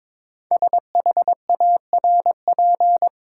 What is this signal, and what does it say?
Morse code, heard as a keyed tone, about 22 wpm, 700 Hz. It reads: SHARP
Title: Sentences from Top 1600 Words - New Word Drill Alt - 22wpm